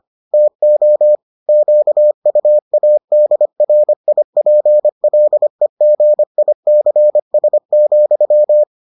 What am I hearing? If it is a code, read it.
Morse code, heard as a keyed tone, about 25 wpm, 600 Hz. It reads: TO QUADRIPLEGICS,